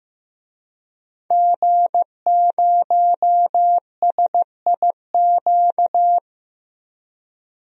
Morse code, heard as a keyed tone, about 15 wpm, 700 Hz. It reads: G0SIQ
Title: DXCC-100 Call Signs - 5 Characters - Repeated 2x - 15wpm